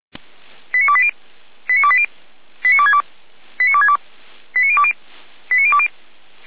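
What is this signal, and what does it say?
Five-tone selective call (ZVEI), heard as quick runs of five tones, tones of about 70 ms. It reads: 89280 89280 89262 89262 89020 89020
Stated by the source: ZVEI five-tone call 89333 received on SDR.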